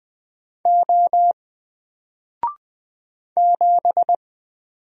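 Morse code, heard as a keyed tone, about 20 wpm, 700 Hz. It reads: O 7
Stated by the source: Single letter and number Morse drill at 20 wpm, from N4LCD.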